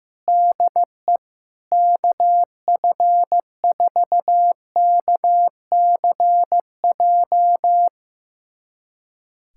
Morse code, heard as a keyed tone, about 15 wpm, 700 Hz. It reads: DE KF4KCJ
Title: USA Callsign DE Callsign - 6 Characters - 15wpm